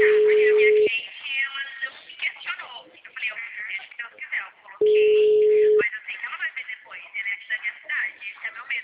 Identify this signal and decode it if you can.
Ringback tone (Europe, 425 Hz), with speech.
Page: Cordeiropolis